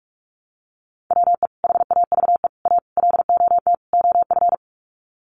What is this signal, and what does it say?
Morse code, heard as a keyed tone, about 44 wpm, 700 Hz. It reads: WE HAVE A LOT OF
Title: Sets of 5 Words 45wpm